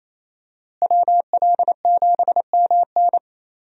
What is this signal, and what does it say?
Morse code, heard as a keyed tone, about 28 wpm, 700 Hz. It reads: WL7MN
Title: USA Call Signs - 5 Characters - Repeated 3x - 28wpm